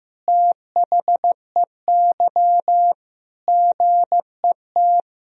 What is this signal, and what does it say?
Morse code, heard as a keyed tone, about 15 wpm, 700 Hz. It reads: THEY GET
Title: Sentences from Top 400 Words and Abbreviations 15wpm